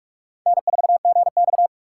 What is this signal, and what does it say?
Morse code, heard as a keyed tone, about 44 wpm, 700 Hz. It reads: N4GX